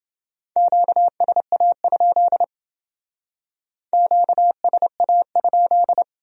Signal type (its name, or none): Morse code